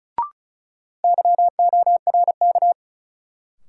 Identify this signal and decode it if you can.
Morse code, heard as a keyed tone, about 36 wpm, 700 Hz. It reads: YORK
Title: Sets of 2 Words - New Word Drill - 35wpm